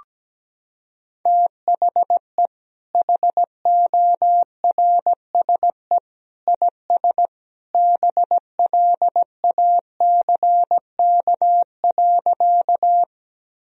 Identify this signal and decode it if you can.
Morse code, heard as a keyed tone, about 17 wpm, 700 Hz. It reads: THE HORSE IS BLACK.
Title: Sentences from Top 1000 Words - Review Part 2 - 17wpm